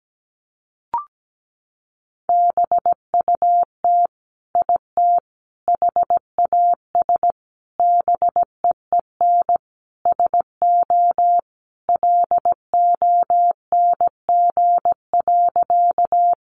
Morse code, heard as a keyed tone, about 17 wpm, 700 Hz. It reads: BUT IT HAS BEEN SO LONG.
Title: Sentences from Top 100 Words 17wpm